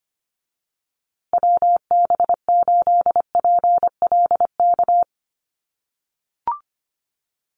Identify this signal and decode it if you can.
Morse code, heard as a keyed tone, about 25 wpm, 700 Hz. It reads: WB8PLK
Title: USA Call Signs Repeated 2x 25wpm